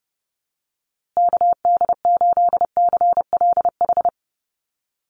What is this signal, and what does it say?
Morse code, heard as a keyed tone, about 30 wpm, 700 Hz. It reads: KD8CLH
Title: USA Call Signs Repeated 2x 30wpm